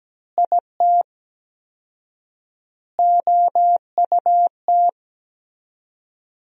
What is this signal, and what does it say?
Morse code, heard as a keyed tone, about 17 wpm, 700 Hz. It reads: IT OUT
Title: 4x Word-Spacing - Sentences from Top 200 Words - 17wpm